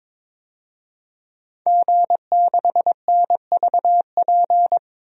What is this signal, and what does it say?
Morse code, heard as a keyed tone, about 21 wpm, 700 Hz. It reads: G6NVP
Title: DXCC-100 Call Signs - 5 Characters - 22wpm